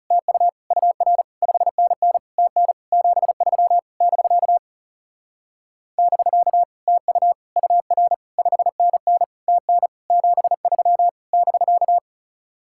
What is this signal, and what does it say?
Morse code, heard as a keyed tone, about 35 wpm, 700 Hz. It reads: TU UR 5NN TN 73 <BK> <BK> TU UR 5NN TN 73 <BK>
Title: Speed-Racing Common QSO Phrases - 30wpm